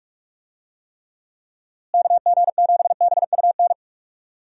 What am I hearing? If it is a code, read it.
Morse code, heard as a keyed tone, about 46 wpm, 700 Hz. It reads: KG7BUN
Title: USA Call Signs - 6 Characters - Repeated 2x - 45wpm